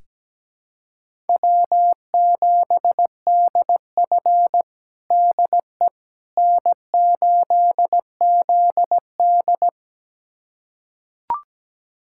Morse code, heard as a keyed tone, about 17 wpm, 700 Hz. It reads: W7DF DE N8ZD
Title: USA Callsign DE Callsign - 4 Characters - 17wpm